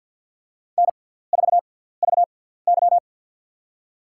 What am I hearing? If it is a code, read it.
Morse code, heard as a keyed tone, about 52 wpm, spaced as for 17 wpm, 700 Hz. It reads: N4VY
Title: USA Call Signs - 6x Character Spacing - 50wpm